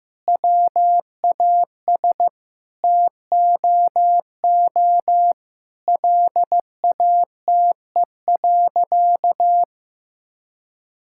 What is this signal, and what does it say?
Morse code, heard as a keyed tone, about 15 wpm, 700 Hz. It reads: WAS TOO LATE.
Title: Sentences from Top 1000 Words - Review Part 6 - 15wpm